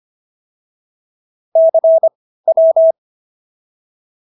Morse code, heard as a keyed tone, about 25 wpm, spaced as for 15 wpm, 650 Hz. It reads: CW